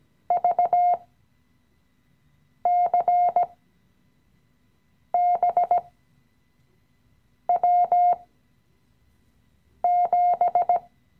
Morse code, heard as a keyed tone, about 17 wpm, spaced as for 5 wpm, 710 Hz. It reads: VCBW7
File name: CW-Online-Kurs / Anfänger / Lektion 017